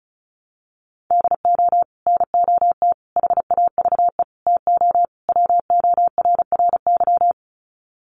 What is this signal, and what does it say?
Morse code, heard as a keyed tone, about 34 wpm, 700 Hz. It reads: DO NOT HAVE TO WORRY
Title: Sets of 5 Words 35wpm